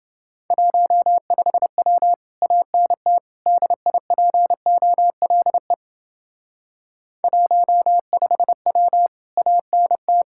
Morse code, heard as a keyed tone, about 29 wpm, 700 Hz. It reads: 15W ANT DIPOLE 15W ANT